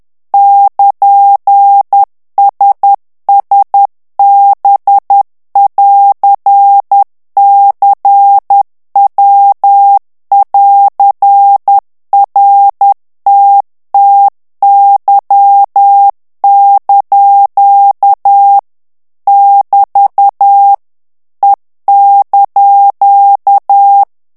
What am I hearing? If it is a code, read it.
Morse code, heard as a keyed tone, about 10 wpm, 800 Hz. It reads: <KN>SSB<AR>CW<AR>RTTY) <BT> E)
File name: ITALIEN texte clair 2 vitesse 10 mots minute